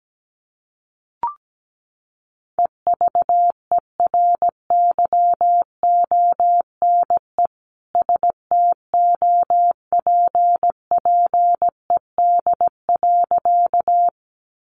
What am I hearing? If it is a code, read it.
Morse code, heard as a keyed tone, about 17 wpm, 700 Hz. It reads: EVERYONE STOPPED.
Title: A Sound Of Thunder 17wpm